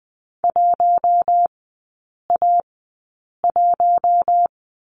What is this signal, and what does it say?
Morse code, heard as a keyed tone, about 20 wpm, spaced as for 8 wpm, 690 Hz. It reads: 1A1
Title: 133 Lesson 22e - Question Mark - Three Characters 20wpm